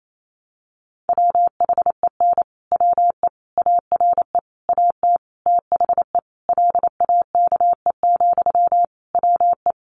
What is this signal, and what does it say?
Morse code, heard as a keyed tone, about 28 wpm, 700 Hz. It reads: WHEN WE ARE AT THE LAKE, WE
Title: Sentences from Top 1100 Words 28wpm